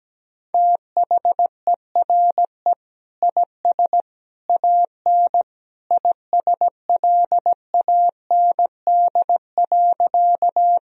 Morse code, heard as a keyed tone, about 17 wpm, 700 Hz. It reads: THERE IS AN ISLAND.
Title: Sentences from Top 1000 Words - Review Part 3 - 17wpm